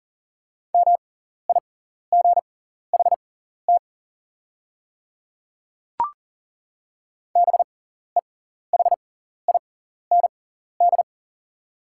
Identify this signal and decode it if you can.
Morse code, heard as a keyed tone, about 40 wpm, spaced as for 14 wpm, 690 Hz. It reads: MIGHT BEHIND
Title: 6x Character Spacing - Top 500 Words - 40wpm